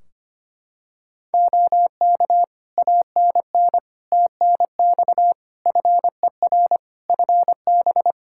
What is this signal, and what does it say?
Morse code, heard as a keyed tone, about 25 wpm, 700 Hz. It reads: OK ANN TNX FER FB
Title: Practice QSO - Exchange 4 - Ending - 25wpm